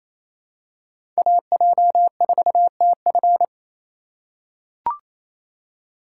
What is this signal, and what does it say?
Morse code, heard as a keyed tone, about 28 wpm, 700 Hz. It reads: AJ4TF
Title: K1USN SST QSO Practice - 28wpm